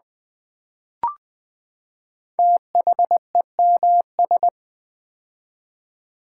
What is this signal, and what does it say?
Morse code, heard as a keyed tone, about 20 wpm, 700 Hz. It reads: THEMS
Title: Build A Word - 5 Letters At A Time - 10-Letter Words - 20wpm